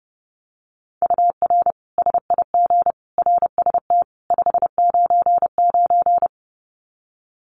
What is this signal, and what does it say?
Morse code, heard as a keyed tone, about 30 wpm, 700 Hz. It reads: UR SIG RST 599
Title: Common QSO Phrases 30wpm